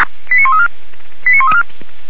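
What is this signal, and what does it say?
Five-tone selective call (ZVEI), heard as quick runs of five tones, tones of about 70 ms. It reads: 89134 89134
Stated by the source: ZVEI five-tone call received by SDR, logged by Feuerwehr Orlamünde fire brigade.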